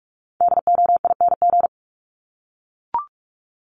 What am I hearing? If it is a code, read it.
Morse code, heard as a keyed tone, about 44 wpm, 700 Hz. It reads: DOING
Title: ING Suffix - Top 100 Words with Base Words 45wpm